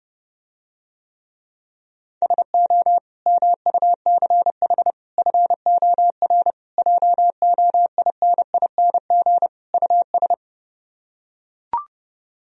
Morse code, heard as a keyed tone, about 32 wpm, 700 Hz. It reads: SO MUCH FOR JOINING US